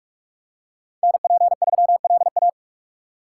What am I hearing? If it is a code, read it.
Morse code, heard as a keyed tone, about 44 wpm, 700 Hz. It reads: NP3LA